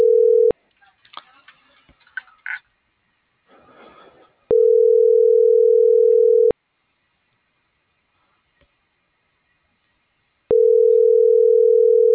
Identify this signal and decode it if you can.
Ringback tone (North America), 440 + 480 Hz.